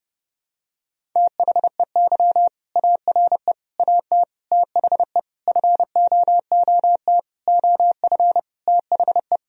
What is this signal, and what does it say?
Morse code, heard as a keyed tone, about 28 wpm, 700 Hz. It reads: THEY ARE AT THE FOOT OF THE
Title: Sentences from Top 1000 Words - Review Part 4 - 30wpm